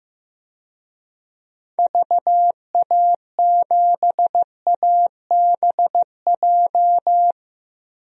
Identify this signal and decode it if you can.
Morse code, heard as a keyed tone, about 15 wpm, 700 Hz. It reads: VA7ABJ
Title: DXCC-100 Call Signs - 6 Characters - 15wpm